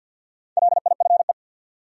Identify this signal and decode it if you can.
Morse code, heard as a keyed tone, about 53 wpm, 700 Hz. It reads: LIFE
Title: Sets of 5 Words - New Word Drill - 50wpm